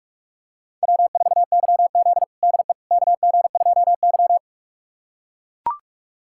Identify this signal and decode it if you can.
Morse code, heard as a keyed tone, about 42 wpm, 700 Hz. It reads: W4YZ DE KG2Y
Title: USA Callsign DE Callsign - 4 Characters - 45wpm